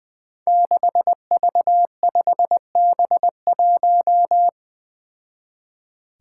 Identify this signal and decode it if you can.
Morse code, heard as a keyed tone, about 21 wpm, 700 Hz. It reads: 6V5B1
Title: Five Challenging Letter-Number Combinations 20wpm